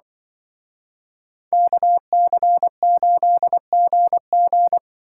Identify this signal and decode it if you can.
Morse code, heard as a keyed tone, about 23 wpm, 700 Hz. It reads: KC8GG
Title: Speed Racing - USA 2x2 Call Signs 20wpm